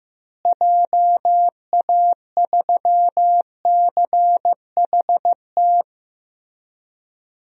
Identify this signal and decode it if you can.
Morse code, heard as a keyed tone, about 15 wpm, 700 Hz. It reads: JA3CHT
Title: DXCC-100 Call Signs - 6 Characters - Repeated 3x - 15wpm